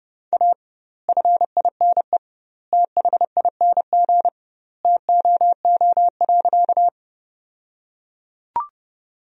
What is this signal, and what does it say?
Morse code, heard as a keyed tone, about 32 wpm, 700 Hz. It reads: A FINE THING TOO.
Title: 2x Word-Spacing - Sentences from Top 500 Words - 30wpm